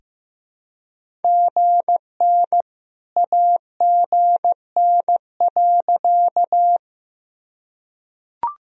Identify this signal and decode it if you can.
Morse code, heard as a keyed tone, about 15 wpm, 700 Hz. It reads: GN AGN.
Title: Sentences from Top 300 Words and Abbreviations 15wpm